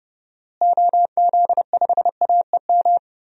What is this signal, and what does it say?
Morse code, heard as a keyed tone, about 30 wpm, 700 Hz. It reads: OZ5AEM
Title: DXCC-100 Call Signs - 6 Characters - Repeated 3x - 30wpm